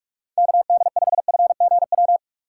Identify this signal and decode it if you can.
Morse code, heard as a keyed tone, about 46 wpm, 700 Hz. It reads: KD5FGW
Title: USA Call Signs - 2x Character Spacing - 45wpm